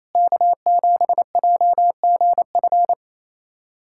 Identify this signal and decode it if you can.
Morse code, heard as a keyed tone, about 28 wpm, 700 Hz. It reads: K7JGF
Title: USA Callsign DE Callsign - 5 Characters - 28wpm